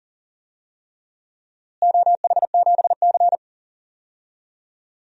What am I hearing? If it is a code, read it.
Morse code, heard as a keyed tone, about 39 wpm, 700 Hz. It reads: OH7C
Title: DXCC-100 Call Signs - 4 Characters - Repeated 3x - 40wpm